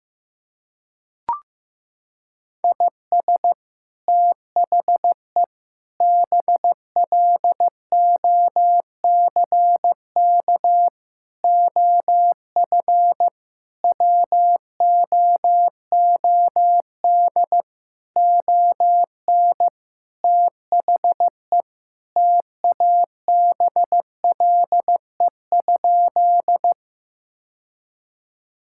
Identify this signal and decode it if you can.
Morse code, heard as a keyed tone, about 15 wpm, 700 Hz. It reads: IS THE BLOCK OF WOOD ON THE TABLE?
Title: Sentences from Top 1600 Words 15wpm